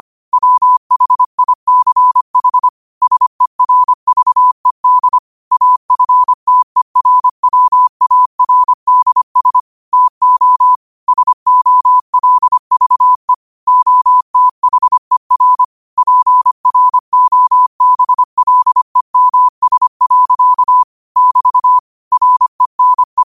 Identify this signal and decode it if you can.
Morse code, heard as a keyed tone, about 25 wpm, 1000 Hz. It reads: WHICH SERVED AFTERWARDS TO SOLVE OTHER PROBLEMS. <BT> RENE